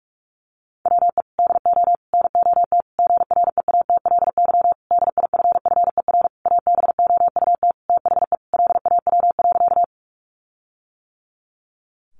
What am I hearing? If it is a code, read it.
Morse code, heard as a keyed tone, about 44 wpm, 700 Hz. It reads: WE DO NOT GREATLY DIFFER ABOUT THE LAW.
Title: Sentences from Top 1400 Words 45wpm